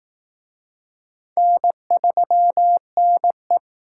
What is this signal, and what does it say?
Morse code, heard as a keyed tone, about 19 wpm, 700 Hz. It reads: N3NE